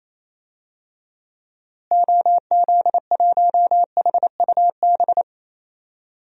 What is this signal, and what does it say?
Morse code, heard as a keyed tone, about 28 wpm, 700 Hz. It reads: OZ1HUB
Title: DXCC-100 Call Signs - 6 Characters - Repeated 2x - 28wpm